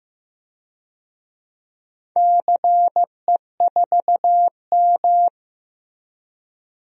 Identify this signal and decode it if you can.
Morse code, heard as a keyed tone, about 15 wpm, 700 Hz. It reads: CE4M
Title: DXCC-100 Call Signs - 4 Characters - 15wpm